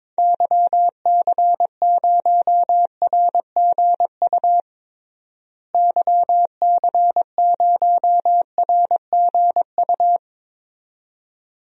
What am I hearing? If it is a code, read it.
Morse code, heard as a keyed tone, about 22 wpm, 700 Hz. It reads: YC0RGU YC0RGU
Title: DXCC-100 Call Signs - 6 Characters - Repeated 2x - 22wpm